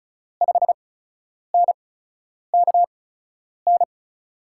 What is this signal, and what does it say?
Morse code, heard as a keyed tone, about 36 wpm, spaced as for 10 wpm, 700 Hz. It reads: 5NKN
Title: USA Call Signs - 6 Characters - 8x Character Spacing - 35wpm